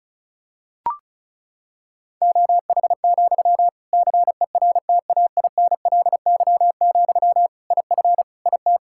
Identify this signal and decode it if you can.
Morse code, heard as a keyed tone, about 36 wpm, 700 Hz. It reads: OH, CERTAINLY, IF IT
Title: Sentences from Top 1600 Words 35wpm